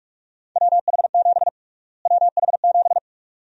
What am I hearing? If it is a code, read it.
Morse code, heard as a keyed tone, about 41 wpm, 700 Hz. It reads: WH7 WH7
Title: USA Call Sign Prefixes - Repeated 2x - 45wpm